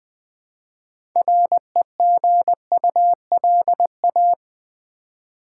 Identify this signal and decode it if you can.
Morse code, heard as a keyed tone, about 20 wpm, 700 Hz. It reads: REGULA
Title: Build A Word - 3 Letters At A Time - 8-Letter Words - 20wpm